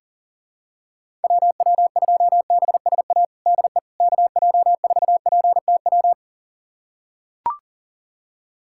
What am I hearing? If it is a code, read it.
Morse code, heard as a keyed tone, about 40 wpm, 700 Hz. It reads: WW2BSA DE KJ4PTW
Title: USA Callsign DE Callsign - 6 Characters - 40wpm